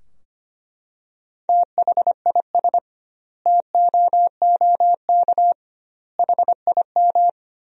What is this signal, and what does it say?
Morse code, heard as a keyed tone, about 25 wpm, 700 Hz. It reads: THIS TOOK HIM
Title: 2x Word-Spacing - Sentences from Top 300 Words - 25wpm